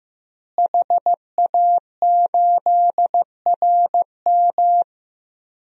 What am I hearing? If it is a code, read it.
Morse code, heard as a keyed tone, about 15 wpm, 700 Hz. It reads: HA8RM
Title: DXCC-100 Call Signs - 5 Characters - Repeated 2x - 15wpm